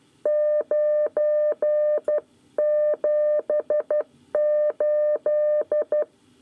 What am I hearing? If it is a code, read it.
Morse code, heard as a keyed tone, about 12 wpm, 600 Hz. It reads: 978
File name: 9 lezione cw 1